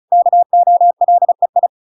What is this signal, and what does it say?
Morse code, heard as a keyed tone, about 34 wpm, 700 Hz. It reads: KOLEI